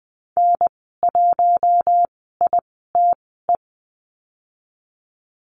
Morse code, heard as a keyed tone, about 20 wpm, spaced as for 14 wpm, 700 Hz. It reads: N1ITE
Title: 069 Lesson 12g - Letter d - Call Signs 20wpm